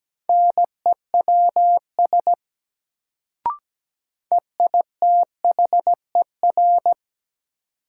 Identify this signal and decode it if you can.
Morse code, heard as a keyed tone, about 17 wpm, 700 Hz. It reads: NEWS EITHER
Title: Sets of 5 Words - New Word Drill - 17wpm